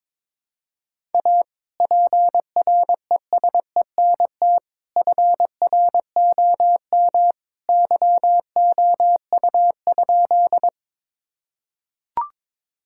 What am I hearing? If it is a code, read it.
Morse code, heard as a keyed tone, about 22 wpm, 700 Hz. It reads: A PRESENT FROM YOU?
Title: Sentences from Top 1000 Words - Review Part 5 - 22wpm